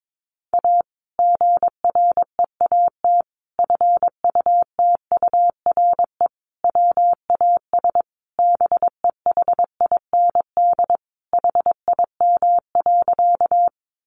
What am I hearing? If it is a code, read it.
Morse code, heard as a keyed tone, about 22 wpm, 700 Hz. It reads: A GREAT FUTURE WAS BEHIND HIM.
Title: Sentences from Top 700 Words 22wpm